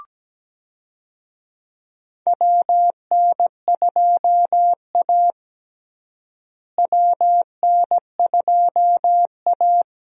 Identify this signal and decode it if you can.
Morse code, heard as a keyed tone, about 17 wpm, 700 Hz. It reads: WN2A WN2A
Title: USA Call Signs - 4 Characters - Repeated 2x - 17wpm